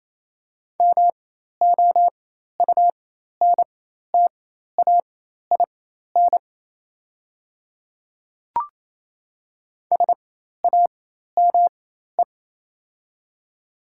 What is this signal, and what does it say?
Morse code, heard as a keyed tone, about 28 wpm, spaced as for 13 wpm, 700 Hz. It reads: MOUNTAIN SAME
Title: 4x Character Spacing - Top 500 Words - 28wpm